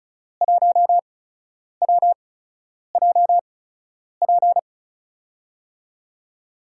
Morse code, heard as a keyed tone, about 36 wpm, spaced as for 10 wpm, 700 Hz. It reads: 1WJP
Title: USA Call Signs - 6 Characters - 8x Character Spacing - 35wpm